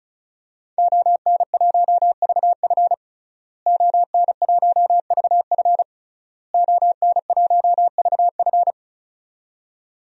Morse code, heard as a keyed tone, about 36 wpm, 700 Hz. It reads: ON1VF ON1VF ON1VF
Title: DXCC-100 Call Signs - 5 Characters - Repeated 3x - 35wpm